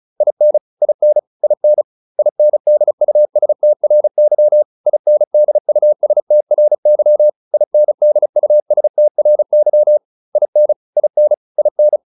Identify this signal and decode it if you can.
Morse code, heard as a keyed tone, about 34 wpm, 600 Hz. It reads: IN IN IN INDUSTRY INDUSTRY INDUSTRY IN IN IN